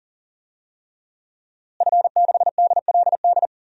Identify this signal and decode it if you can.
Morse code, heard as a keyed tone, about 37 wpm, 700 Hz. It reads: F6DLD
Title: DXCC-100 Call Signs - 5 Characters - Repeated 3x - 40wpm